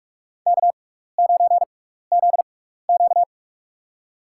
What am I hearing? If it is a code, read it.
Morse code, heard as a keyed tone, about 46 wpm, spaced as for 16 wpm, 700 Hz. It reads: K9ZQ
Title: USA Call Signs - 6x Character Spacing - 45wpm